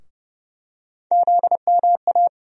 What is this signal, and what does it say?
Morse code, heard as a keyed tone, about 30 wpm, 700 Hz. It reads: ZMA
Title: USA Call Sign Postfixes - 3 Letters 30wpm